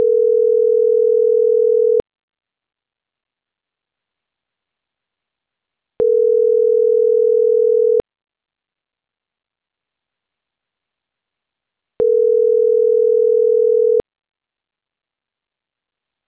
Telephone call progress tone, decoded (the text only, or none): ringback tone